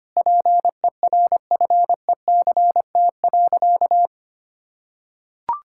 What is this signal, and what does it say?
Morse code, heard as a keyed tone, about 25 wpm, 700 Hz. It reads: PERFECT.
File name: practice / 25wpm / Sentences with 9-Word Phrases - 25wpm